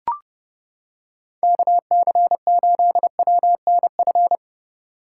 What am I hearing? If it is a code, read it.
Morse code, heard as a keyed tone, about 30 wpm, 700 Hz. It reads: KC8WNF